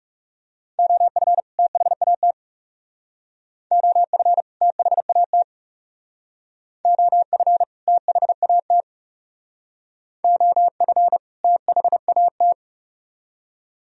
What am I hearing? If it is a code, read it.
Morse code, heard as a keyed tone, about 36 wpm, 700 Hz. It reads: OF THAT OF THAT OF THAT OF THAT
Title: Speed-Racing - Sets of 2 Words 30wpm